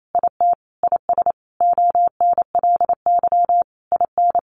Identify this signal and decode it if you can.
Morse code, heard as a keyed tone, about 28 wpm, 700 Hz. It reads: IT IS ONLY IN
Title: Sentences from Top 1300 Words 28wpm